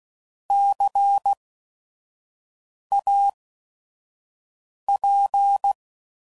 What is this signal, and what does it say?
Morse code, heard as a keyed tone, about 16 wpm, spaced as for 5 wpm, 800 Hz. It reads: CAP
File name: Prefixes / By First Letter / Cs